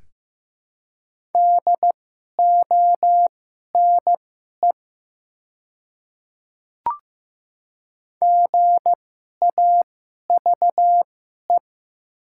Morse code, heard as a keyed tone, about 15 wpm, spaced as for 11 wpm, 700 Hz. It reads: DONE GAVE
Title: 2x Character Spacing - Top 500 Words - 15wpm